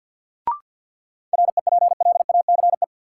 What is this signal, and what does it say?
Morse code, heard as a keyed tone, about 53 wpm, 700 Hz. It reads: REPLACE
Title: Sentences from Top 1400 Words - New Word Drill - 50wpm